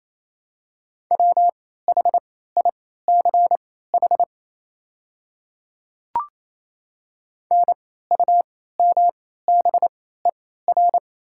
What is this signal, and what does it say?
Morse code, heard as a keyed tone, about 28 wpm, spaced as for 16 wpm, 700 Hz. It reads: WHICH NUMBER
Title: 3x Character Spacing - Top 100 Words - 28wpm